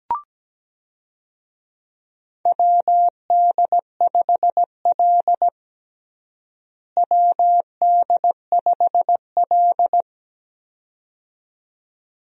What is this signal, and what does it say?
Morse code, heard as a keyed tone, about 17 wpm, 700 Hz. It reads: WD5L WD5L
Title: USA Call Signs - 4 Characters - Repeated 2x - 17wpm